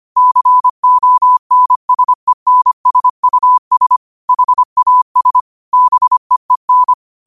Morse code, heard as a keyed tone, about 25 wpm, 1000 Hz. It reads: CONSENSUS HAS BEEN
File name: qotdfiles / 25wpm